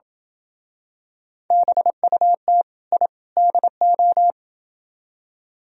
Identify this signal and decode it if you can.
Morse code, heard as a keyed tone, about 28 wpm, 700 Hz. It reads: BUT I DO